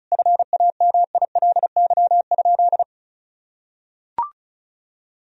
Morse code, heard as a keyed tone, about 36 wpm, 700 Hz. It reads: FAMILY?
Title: Sentences from Top 1000 Words - Review Part 4 - 35wpm